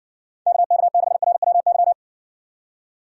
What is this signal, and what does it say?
Morse code, heard as a keyed tone, about 60 wpm, 700 Hz. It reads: KK6RWY